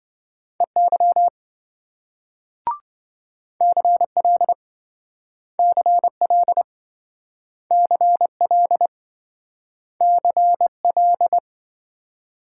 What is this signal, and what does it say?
Morse code, heard as a keyed tone, about 26 wpm, 700 Hz. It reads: EY CL CL CL CL